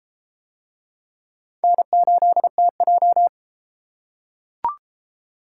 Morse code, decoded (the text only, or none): N8TJ